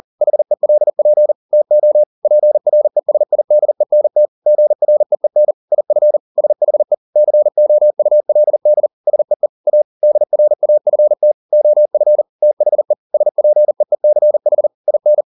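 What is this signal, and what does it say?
Morse code, heard as a keyed tone, about 40 wpm, 600 Hz. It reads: HELP TO PRESIDENT GREEN IF SHE COULD SEE A DRAFT OF THE SPEECH IN
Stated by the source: Morse code at 40 wpm, practice text, CW med Gnister.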